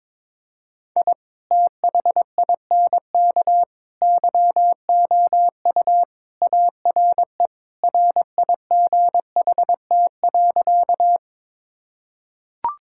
Morse code, heard as a keyed tone, about 22 wpm, 700 Hz. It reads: I THINK YOU ARE RIGHT.
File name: practice / 22wpm / Sentences with 3-Word Phrases - 22wpm